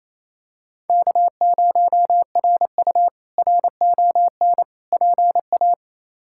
Morse code, heard as a keyed tone, about 28 wpm, 700 Hz. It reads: K0RU RON PA
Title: K1USN SST QSO Practice - 28wpm